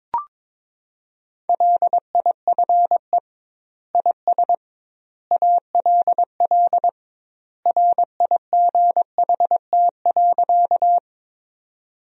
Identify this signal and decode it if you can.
Morse code, heard as a keyed tone, about 22 wpm, 700 Hz. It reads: LIFE IS ALL RIGHT.
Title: 2x Word-Spacing - Sentences from Top 300 Words - 22wpm